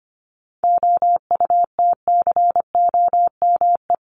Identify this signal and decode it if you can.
Morse code, heard as a keyed tone, about 25 wpm, 700 Hz. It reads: OUTCOME
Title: Top 1501-1750 Words 25wpm